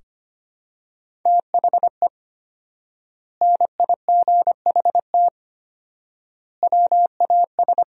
Morse code, heard as a keyed tone, about 25 wpm, 700 Hz. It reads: THE NIGHT WAS
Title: 4x Word-Spacing - Sentences from Top 500 Words - 25wpm